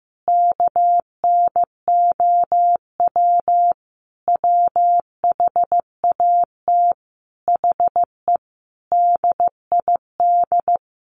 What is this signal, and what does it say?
Morse code, heard as a keyed tone, about 14 wpm, 700 Hz. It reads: KNOW WHAT HE DID
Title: Sentences from Top 300 Words 15wpm